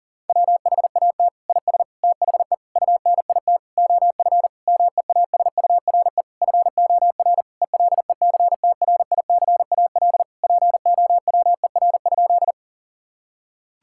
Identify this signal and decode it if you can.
Morse code, heard as a keyed tone, about 39 wpm, 700 Hz. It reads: WHAT IS THE UNIT OF MEASURE FOR ELECTRICAL POWER?
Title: Sentences from Top 1000 Words - Review Part 6 - 40wpm